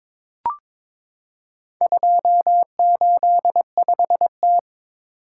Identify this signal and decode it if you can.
Morse code, heard as a keyed tone, about 22 wpm, 700 Hz. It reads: 285T
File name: practice / 22wpm / Four Letter-Number Combinations - 22wpm